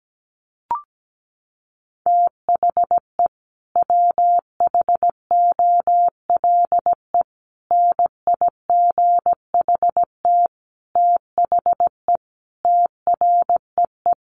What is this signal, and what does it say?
Morse code, heard as a keyed tone, about 17 wpm, 700 Hz. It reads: THE WHOLE NIGHT THE TREE
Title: Sentences from Top 500 Words 17wpm